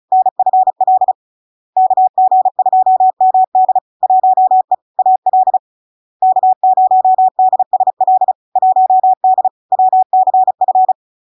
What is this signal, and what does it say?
Morse code, heard as a keyed tone, about 36 wpm, 750 Hz. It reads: NFL KG2MD 1E AL K0DSL 1D WCF
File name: LARK-FD-2022-Log-in-CW-35wpm-750-Hz